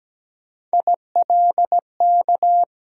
Morse code, heard as a keyed tone, about 17 wpm, 700 Hz. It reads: ILK
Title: USA Call Sign Postfixes - 3 Letters 17wpm